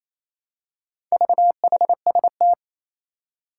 Morse code, heard as a keyed tone, about 27 wpm, 700 Hz. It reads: VHST
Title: Four Challenging Letter-Number Combinations 28wpm